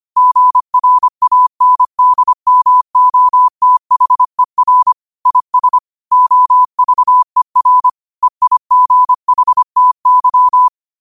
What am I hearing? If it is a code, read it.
Morse code, heard as a keyed tone, about 25 wpm, 1000 Hz. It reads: GRANDMOTHER IS OVER EIGHTY
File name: 25wpm